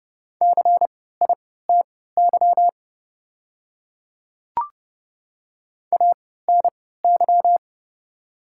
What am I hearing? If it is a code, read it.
Morse code, heard as a keyed tone, about 30 wpm, spaced as for 17 wpm, 700 Hz. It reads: CITY ANY
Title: 3x Character Spacing - Top 300 Words - 30wpm